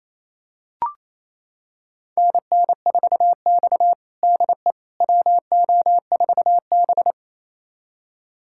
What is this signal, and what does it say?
Morse code, heard as a keyed tone, about 28 wpm, 700 Hz. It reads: NN4X DE WO4B
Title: USA Callsign DE Callsign - 4 Characters - 28wpm